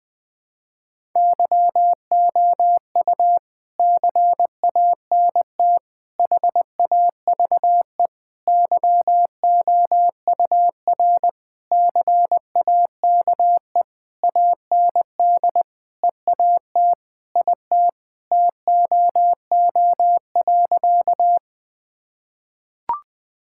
Morse code, heard as a keyed tone, about 21 wpm, 700 Hz. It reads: YOU CANT HAVE YOUR CAKE AND EAT IT TOO.
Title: Sentences with 9-Word Phrases - Spoken Beginning - 20wpm